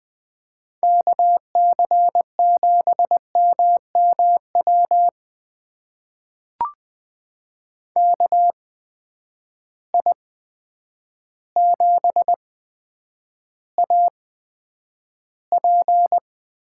Morse code, heard as a keyed tone, about 20 wpm, spaced as for 9 wpm, 700 Hz. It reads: KC7MMW KI7AP
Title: USA Call Signs - 6 Characters - 8x Character Spacing - 20wpm